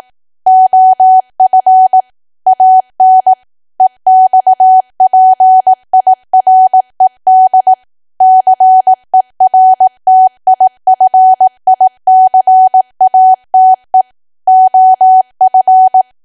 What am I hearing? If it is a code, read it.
Morse code, heard as a keyed tone, about 18 wpm, 750 Hz. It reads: OF AN EXPIRED CERTIFICATE OF
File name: cwa / W1AW-Level2 / 18WPM